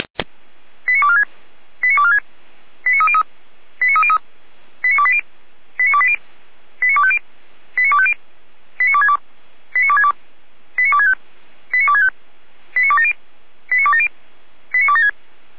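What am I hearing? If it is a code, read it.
Five-tone selective call (ZVEI), heard as quick runs of five tones, tones of about 80 ms. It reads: 89246 89246 89303 89303 89280 89280 89250 89250 89262 89262 89265 89265 89270 89270 89276
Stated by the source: ZVEI five-tone call received on an SDR.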